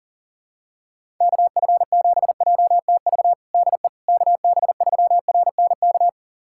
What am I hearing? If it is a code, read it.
Morse code, heard as a keyed tone, about 37 wpm, 700 Hz. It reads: KF7JTV DE KB3RNK
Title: USA Callsign DE Callsign 40wpm